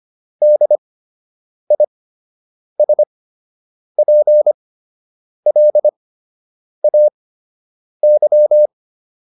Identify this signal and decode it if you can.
Morse code, heard as a keyed tone, about 25 wpm, spaced as for 8 wpm, 600 Hz. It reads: DISPLAY